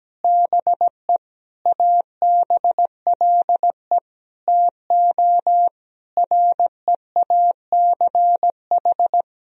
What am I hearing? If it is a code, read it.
Morse code, heard as a keyed tone, about 17 wpm, 700 Hz. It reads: BE ABLE TO REACH
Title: Sentences from Top 1000 Words - Review Part 2 - 17wpm